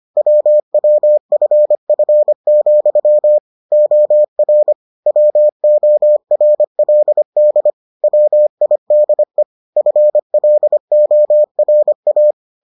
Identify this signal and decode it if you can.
Morse code, heard as a keyed tone, about 25 wpm, 600 Hz. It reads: WWFF, OR WORLD WIDE FLORA